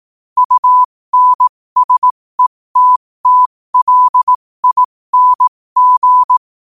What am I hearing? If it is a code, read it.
Morse code, heard as a keyed tone, about 18 wpm, spaced as for 15 wpm, 1000 Hz. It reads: UNSETTLING
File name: qotdfiles / 15wpm